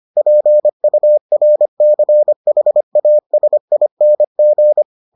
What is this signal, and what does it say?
Morse code, heard as a keyed tone, about 25 wpm, 600 Hz. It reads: PURCHASING